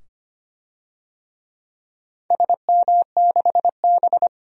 Morse code, heard as a keyed tone, about 25 wpm, 700 Hz. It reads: SM6B
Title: DXCC-100 Call Signs - 4 Characters - 25wpm